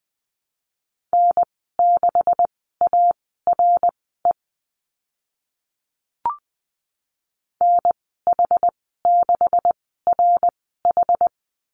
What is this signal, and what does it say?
Morse code, heard as a keyed tone, about 20 wpm, spaced as for 14 wpm, 700 Hz. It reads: N6ARE NH6RH
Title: 140 Lesson 23g - Letter f - Call Signs 20wpm